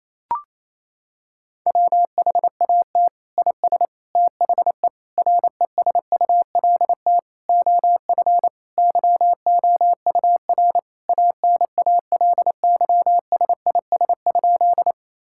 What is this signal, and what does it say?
Morse code, heard as a keyed tone, about 28 wpm, 700 Hz. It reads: WHAT IS THE RESULT OF YOUR ANALYSIS?
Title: Sentences from Top 1000 Words - Review Part 5 - 28wpm